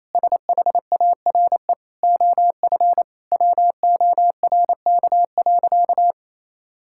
Morse code, heard as a keyed tone, about 28 wpm, 700 Hz. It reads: SHARE OF WORK.
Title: Sentences from Top 1000 Words - Review Part 6 - 28wpm